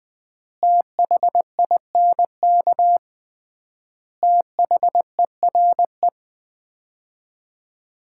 Morse code, heard as a keyed tone, about 20 wpm, 700 Hz. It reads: THINK THERE
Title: 3x Word-Spacing - Sets of 5 Words - 20wpm